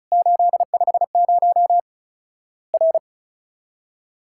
Morse code, heard as a keyed tone, about 36 wpm, 700 Hz. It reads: 850 R
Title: Worked All Europe DX Contest - QTC Exchange - 35wpm